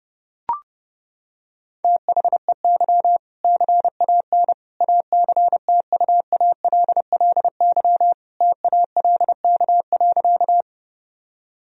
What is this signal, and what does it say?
Morse code, heard as a keyed tone, about 30 wpm, 700 Hz. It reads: THEY CAN ACTUALLY TALK.
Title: Sentences from Top 1000 Words - Review Part 4 - 30wpm